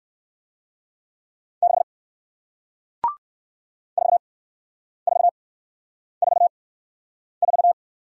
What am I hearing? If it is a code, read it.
Morse code, heard as a keyed tone, about 54 wpm, spaced as for 9 wpm, 700 Hz. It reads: 6 4444
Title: Speed-Racing Single Letter-Number 45wpm